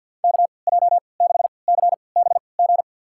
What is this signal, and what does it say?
Morse code, heard as a keyed tone, about 49 wpm, spaced as for 28 wpm, 700 Hz. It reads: KJ6CBG
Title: USA Call Signs - 6 Characters - 3x Character Spacing - 50wpm